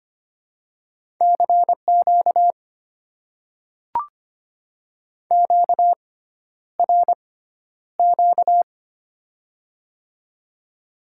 Morse code, heard as a keyed tone, about 25 wpm, spaced as for 11 wpm, 700 Hz. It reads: CQ QRQ
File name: Top 90 QSO Elements - 6x Character Spacing - 25wpm